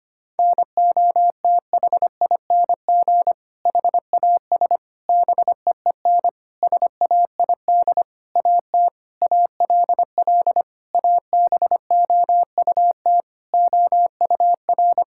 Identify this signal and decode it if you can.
Morse code, heard as a keyed tone, about 25 wpm, 700 Hz. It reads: NOTHING HAS BEEN SAID AT ALL ABOUT OUR
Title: Sentences from Top 1000 Words - Review Part 5 - 25wpm